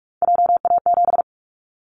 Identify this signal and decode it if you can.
Morse code, heard as a keyed tone, about 44 wpm, 700 Hz. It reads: JA7
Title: Callsign Prefixes for Top 50 DXCC - 45wpm